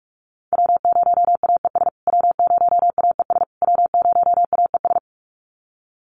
Morse code, heard as a keyed tone, about 44 wpm, 700 Hz. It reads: W0AES W0AES W0AES
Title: USA Call Signs Repeated 3x 45wpm